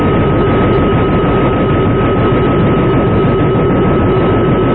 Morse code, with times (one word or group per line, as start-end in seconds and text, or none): none